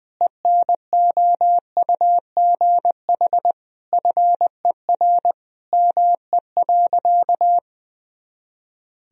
Morse code, heard as a keyed tone, about 20 wpm, 700 Hz. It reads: ENOUGH FER ME.
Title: Sentences from Top 400 Words and Abbreviations 20wpm